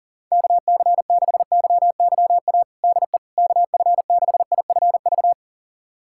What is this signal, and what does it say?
Morse code, heard as a keyed tone, about 40 wpm, 700 Hz. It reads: KC6YYA DE KF6IFV
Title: USA Callsign DE Callsign - 6 Characters - 40wpm